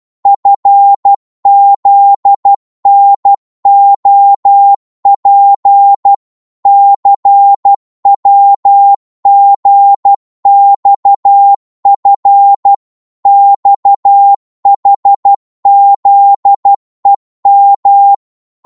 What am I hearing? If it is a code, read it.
Morse code, heard as a keyed tone, about 12 wpm, 800 Hz. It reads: FZNOP CWGXF XHZEM
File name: Morse / ltr-12wpm